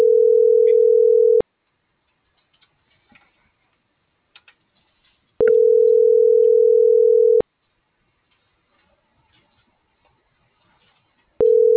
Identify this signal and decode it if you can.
Ringback tone (North America), 440 + 480 Hz.